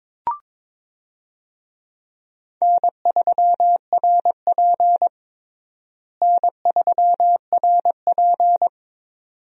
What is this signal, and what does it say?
Morse code, heard as a keyed tone, about 22 wpm, 700 Hz. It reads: N3RP N3RP
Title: USA Call Signs - 4 Characters - Repeated 2x - 22wpm